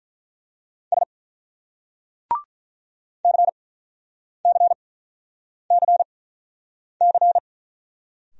Morse code, heard as a keyed tone, about 46 wpm, spaced as for 9 wpm, 700 Hz. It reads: S CCCC